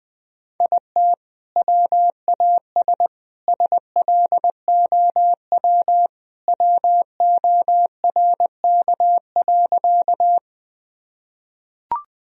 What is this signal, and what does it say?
Morse code, heard as a keyed tone, about 20 wpm, 700 Hz. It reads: IT WAS SLOW WORK.
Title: Sentences from Top 1000 Words - Review Part 2 - 20wpm